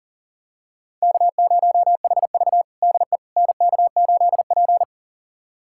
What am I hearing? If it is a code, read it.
Morse code, heard as a keyed tone, about 40 wpm, 700 Hz. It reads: K0HV DE NK8P
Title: USA Callsign DE Callsign - 4 Characters - 40wpm